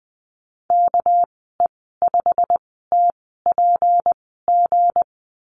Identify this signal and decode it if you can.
Morse code, heard as a keyed tone, about 20 wpm, spaced as for 14 wpm, 700 Hz. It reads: KE5TPG
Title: 207 Lesson 33g - Letter j - Call Signs 20wpm